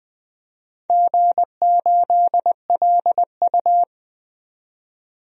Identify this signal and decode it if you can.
Morse code, heard as a keyed tone, about 20 wpm, 700 Hz. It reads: G8LU
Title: DXCC-100 Call Signs - 4 Characters - 20wpm